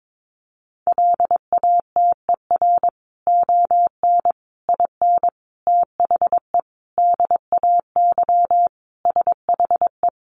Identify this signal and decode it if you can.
Morse code, heard as a keyed tone, about 22 wpm, 700 Hz. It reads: LATER ON IN THE DAY SHE
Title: Sentences from Top 700 Words 22wpm